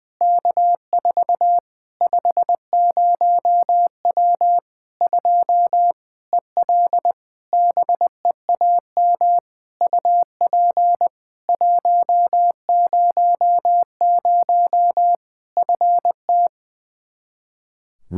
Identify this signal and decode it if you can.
Morse code, heard as a keyed tone, about 20 wpm, 700 Hz. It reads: K4 50W 2 EL BEAM UP 100 FT
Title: Practice QSO - Exchange 2 - Info - 20wpm